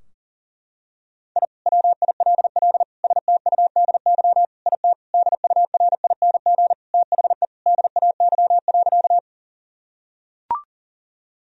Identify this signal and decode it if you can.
Morse code, heard as a keyed tone, about 40 wpm, 700 Hz. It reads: I WILL STUDY IT DURING THE DAY.
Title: Sentences from Top 1000 Words - Review Part 2 - 40wpm